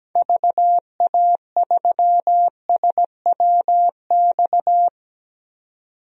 Morse code, heard as a keyed tone, about 17 wpm, 700 Hz. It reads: VA3SWX
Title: DXCC-100 Call Signs - 6 Characters - 17wpm